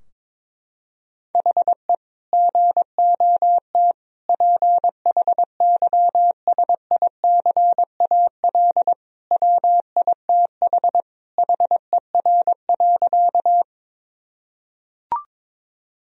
Morse code, heard as a keyed tone, about 21 wpm, 700 Hz. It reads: HE GOT PHYSICAL WITH HER.
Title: Sentences from Top 1000 Words - Review Part 4 - 22wpm